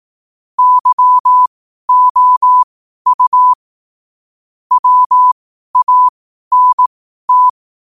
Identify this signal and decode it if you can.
Morse code, heard as a keyed tone, about 18 wpm, spaced as for 12 wpm, 1000 Hz. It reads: YOU WANT
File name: qotdfiles / 12wpm